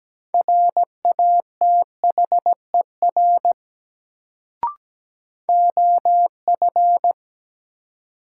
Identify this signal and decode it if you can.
Morse code, heard as a keyed tone, about 17 wpm, 700 Hz. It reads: RATHER OF
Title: Sets of 2 Words - New Word Drill - 17wpm